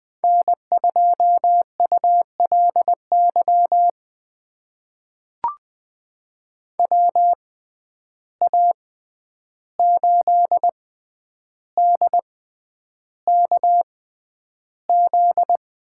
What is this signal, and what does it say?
Morse code, heard as a keyed tone, about 20 wpm, spaced as for 10 wpm, 700 Hz. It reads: N2ULY WA8DKZ
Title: USA Call Signs - 6x Character Spacing - 20wpm